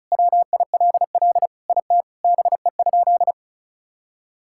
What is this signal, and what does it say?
Morse code, heard as a keyed tone, about 36 wpm, 700 Hz. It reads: WILL IT BE?
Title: Sentences from Top 1000 Words - Review Part 5 - 35wpm